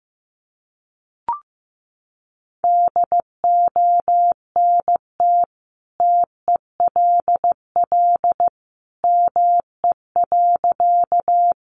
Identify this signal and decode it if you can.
Morse code, heard as a keyed tone, about 15 wpm, 700 Hz. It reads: DONT TELL ME.
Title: Sentences from Top 300 Words 15wpm